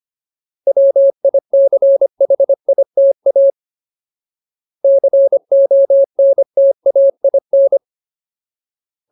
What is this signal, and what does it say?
Morse code, heard as a keyed tone, about 25 wpm, 550 Hz. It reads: WICHITA CONTAIN
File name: Ultimate Word List / 25WPM Ultimate